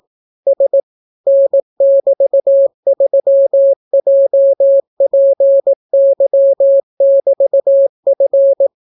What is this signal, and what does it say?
Morse code, heard as a keyed tone, about 18 wpm, 550 Hz. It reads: S N<BT>3JPY<BT>F